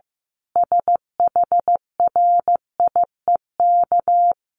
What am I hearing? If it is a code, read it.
Morse code, heard as a keyed tone, about 14 wpm, 700 Hz. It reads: SHRIEK
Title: A Sound Of Thunder 15wpm